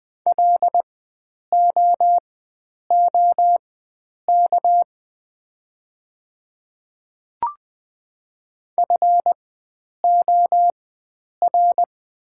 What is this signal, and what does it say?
Morse code, heard as a keyed tone, about 20 wpm, spaced as for 9 wpm, 700 Hz. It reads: LOOK FOR
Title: 4x Character Spacing - Top 100 Words - 20wpm